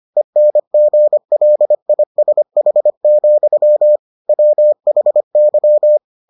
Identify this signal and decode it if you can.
Morse code, heard as a keyed tone, about 25 wpm, 600 Hz. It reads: ENGLISH, WHY